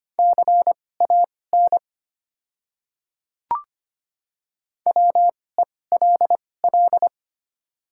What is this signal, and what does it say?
Morse code, heard as a keyed tone, about 25 wpm, spaced as for 18 wpm, 700 Hz. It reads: CAN WELL
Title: 2x Character Spacing - Top 200 Words - 25wpm